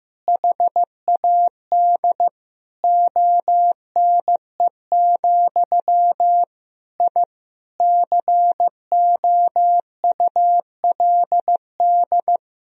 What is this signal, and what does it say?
Morse code, heard as a keyed tone, about 15 wpm, 700 Hz. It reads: HAD ONE, I COULD
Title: Sentences from Top 1000 Words - Review Part 3 - 15wpm